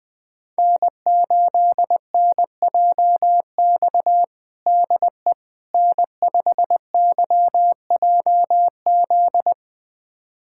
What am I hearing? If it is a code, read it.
Morse code, heard as a keyed tone, about 20 wpm, 700 Hz. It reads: N8NJX DE N5YJZ
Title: USA Callsign DE Callsign - 5 Characters - 20wpm